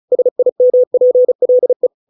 Morse code, heard as a keyed tone, about 34 wpm, 500 Hz. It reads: SIMPLE